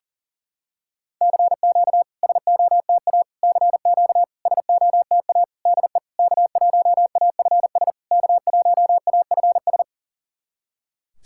Morse code, heard as a keyed tone, about 40 wpm, 690 Hz. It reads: CQ SOTA CQ SOTA DE K1AFS K1AFS
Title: SOTA QSO Practice 40wpm